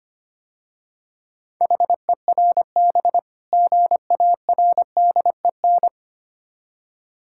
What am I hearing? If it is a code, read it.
Morse code, heard as a keyed tone, about 25 wpm, 700 Hz. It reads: HERB GARDEN
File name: practice / 25wpm / Difficult Words - Speed-Racing with Hints - 25wpm